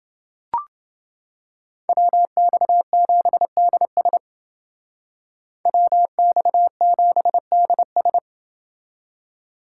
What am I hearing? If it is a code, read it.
Morse code, heard as a keyed tone, about 29 wpm, 700 Hz. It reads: WX7DS WX7DS